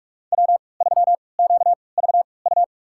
Morse code, heard as a keyed tone, about 46 wpm, spaced as for 25 wpm, 700 Hz. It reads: W3QVU
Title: USA Call Signs - 5 Characters - 3x Character Spacing - 45wpm